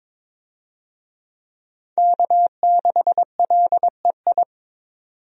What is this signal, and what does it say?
Morse code, heard as a keyed tone, about 22 wpm, 700 Hz. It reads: K6LEI
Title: USA Call Signs - 5 Characters - 22wpm